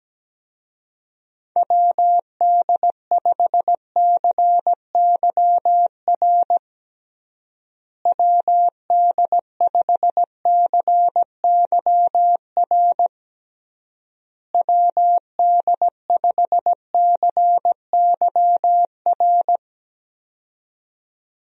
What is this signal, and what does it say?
Morse code, heard as a keyed tone, about 17 wpm, 700 Hz. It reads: WD5CYR WD5CYR WD5CYR
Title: USA Call Signs - 6 Characters - Repeated 3x - 17wpm